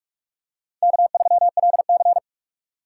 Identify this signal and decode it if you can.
Morse code, heard as a keyed tone, about 45 wpm, 700 Hz. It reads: K3LC